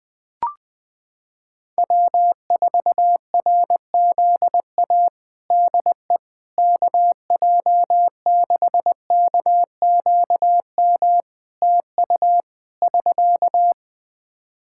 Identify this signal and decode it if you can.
Morse code, heard as a keyed tone, about 19 wpm, 700 Hz. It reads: W4RZA DE KJ6KQM TU <SK>
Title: Practice QSO - Exchange 4 - Ending - 20wpm